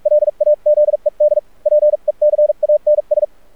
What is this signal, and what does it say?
Morse code, heard as a keyed tone, about 44 wpm, spaced as for 38 wpm, 600 Hz, with radio noise. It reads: LAZED PECANS
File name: archived / CRSnet